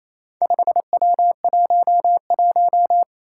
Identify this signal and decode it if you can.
Morse code, heard as a keyed tone, about 28 wpm, 700 Hz. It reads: 5W11
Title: Four Challenging Letter-Number Combinations 28wpm